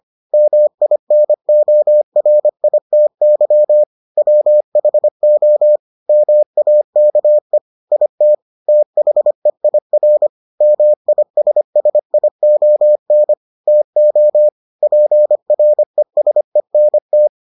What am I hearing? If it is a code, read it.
Morse code, heard as a keyed tone, about 25 wpm, 600 Hz. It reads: MINORITY WHO MAKE IT THEIR MISSION TO PRESENT